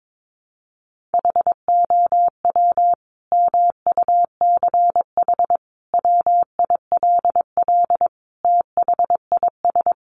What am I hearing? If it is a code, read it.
Morse code, heard as a keyed tone, about 22 wpm, 700 Hz. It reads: HOW MUCH WILL THIS
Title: Sentences from Top 1400 Words 22wpm